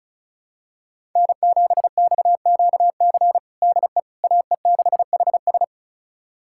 Morse code, heard as a keyed tone, about 36 wpm, 700 Hz. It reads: N7XQC DE AE6HS
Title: USA Callsign DE Callsign - 5 Characters - 35wpm